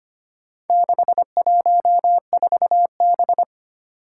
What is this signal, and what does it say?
Morse code, heard as a keyed tone, about 25 wpm, 700 Hz. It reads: 614B
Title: Four Challenging Letter-Number Combinations 25wpm